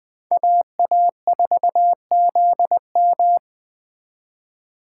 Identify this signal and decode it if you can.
Morse code, heard as a keyed tone, about 20 wpm, 700 Hz. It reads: AA4ZM
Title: Speed Racing - USA Call Signs 17wpm